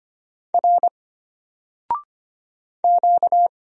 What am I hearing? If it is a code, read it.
Morse code, heard as a keyed tone, about 25 wpm, 700 Hz. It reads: R Q
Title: Single Character (Letter:Number:Punctuation:Prosign) Alt 25wpm